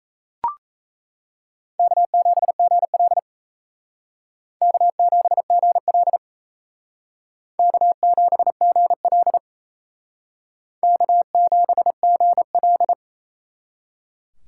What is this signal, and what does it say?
Morse code, heard as a keyed tone, about 35 wpm, 700 Hz. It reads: K7GL K7GL K7GL K7GL
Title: Speed Racing - USA 1x2 Call Signs 28wpm